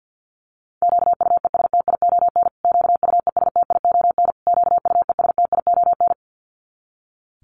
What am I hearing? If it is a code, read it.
Morse code, heard as a keyed tone, about 50 wpm, 700 Hz. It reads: QUESTION QUESTION QUESTION
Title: Top 500 Words Repeated 50wpm